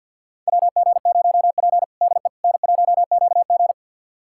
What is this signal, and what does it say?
Morse code, heard as a keyed tone, about 52 wpm, 700 Hz. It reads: WG0P DE N1QG
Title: USA Callsign DE Callsign - 4 Characters - 50wpm